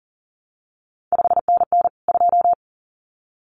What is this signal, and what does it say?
Morse code, heard as a keyed tone, about 39 wpm, 700 Hz. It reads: 5NN 2
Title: CQ WW DX Contest Exchange Encore - 40wpm